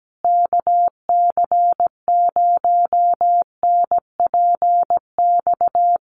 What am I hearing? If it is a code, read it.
Morse code, heard as a keyed tone, about 17 wpm, 700 Hz. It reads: KC0NPX
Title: USA Call Signs Repeated 2x 17wpm